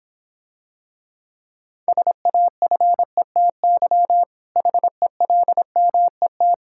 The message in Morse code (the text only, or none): SAFETY HELMET